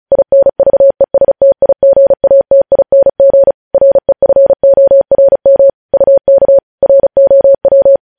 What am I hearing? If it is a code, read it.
Morse code, heard as a keyed tone, about 36 wpm, 550 Hz. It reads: INVESTIGATING REFORM UK ROW
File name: podcast / News headlines-1x-35x35